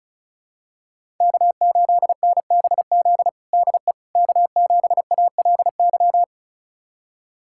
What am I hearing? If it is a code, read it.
Morse code, heard as a keyed tone, about 36 wpm, 700 Hz. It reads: K8NBZ DE K7ALY
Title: USA Callsign DE Callsign - 5 Characters - 35wpm